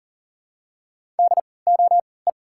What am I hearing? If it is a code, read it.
Morse code, heard as a keyed tone, about 40 wpm, spaced as for 23 wpm, 700 Hz. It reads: DOE
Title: 3x Character Spacing - Top 200 Words - 40wpm